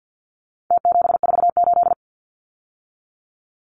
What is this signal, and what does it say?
Morse code, heard as a keyed tone, about 48 wpm, 700 Hz. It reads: T748
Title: Numbers - Contest Style - 4-Digit 50wpm